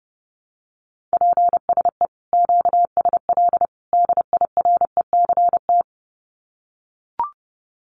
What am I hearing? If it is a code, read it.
Morse code, heard as a keyed tone, about 30 wpm, 700 Hz. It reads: PSE QSL DIRECT
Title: Common QSO Phrases 30wpm